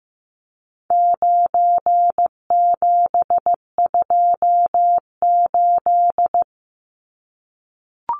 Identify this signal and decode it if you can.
Morse code, heard as a keyed tone, about 15 wpm, 700 Hz. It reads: 9728
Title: Numbers - 4-Digit 15wpm